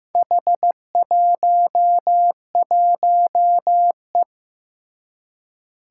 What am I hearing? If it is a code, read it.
Morse code, heard as a keyed tone, about 15 wpm, 700 Hz. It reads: H11E
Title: Four Challenging Letter-Number Combinations 15wpm